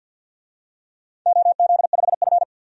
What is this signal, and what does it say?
Morse code, heard as a keyed tone, about 48 wpm, 700 Hz. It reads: OZ5F
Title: DXCC-100 Call Signs - 4 Characters - 50wpm